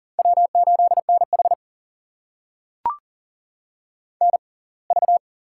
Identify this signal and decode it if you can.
Morse code, heard as a keyed tone, about 39 wpm, spaced as for 27 wpm, 700 Hz. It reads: W8NH NV
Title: USA Call Signs - 4 Characters - 6x Character Spacing - 40wpm